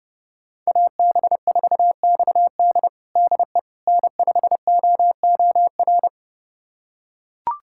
Morse code, heard as a keyed tone, about 32 wpm, 700 Hz. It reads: AB4XD DE N5OOR
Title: USA Callsign DE Callsign - 5 Characters - 30wpm